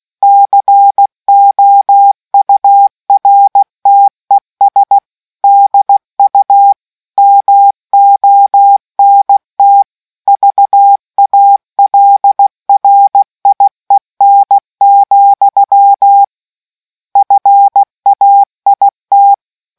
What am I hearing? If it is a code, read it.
Morse code, heard as a keyed tone, about 16 wpm, 800 Hz. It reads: COURTES DU MONT VALRIEN, FAIT